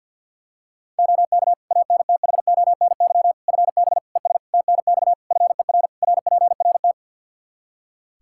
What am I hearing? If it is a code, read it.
Morse code, heard as a keyed tone, about 51 wpm, 700 Hz. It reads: OK ANTHONY FB ES TNX FER RPRT
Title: Practice QSO - Exchange 2 - Info - 50wpm